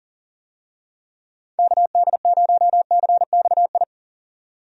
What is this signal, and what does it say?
Morse code, heard as a keyed tone, about 40 wpm, 700 Hz. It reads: KD0CXI
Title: USA Call Signs - 6 Characters - 40wpm